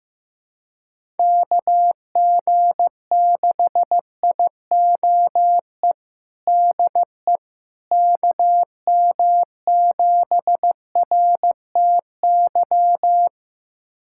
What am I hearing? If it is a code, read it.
Morse code, heard as a keyed tone, about 15 wpm, 700 Hz. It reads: KG6IOE DE KM7RTY
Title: USA Callsign DE Callsign 15wpm